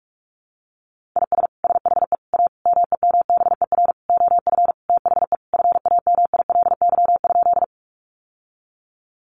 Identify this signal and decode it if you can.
Morse code, heard as a keyed tone, about 44 wpm, 700 Hz. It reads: IS SHE A MEMBER OF THE FAMILY?
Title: Sentences from Top 1100 Words 45wpm